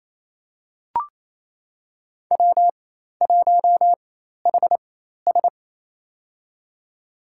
Morse code, heard as a keyed tone, about 28 wpm, spaced as for 13 wpm, 700 Hz. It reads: W1HS
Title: USA Call Signs - 4 Characters - 4x Character Spacing - 28wpm